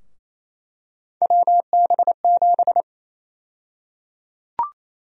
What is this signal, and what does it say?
Morse code, heard as a keyed tone, about 28 wpm, 700 Hz. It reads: WB7
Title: USA Call Sign Prefixes - Repeated 2x - 28wpm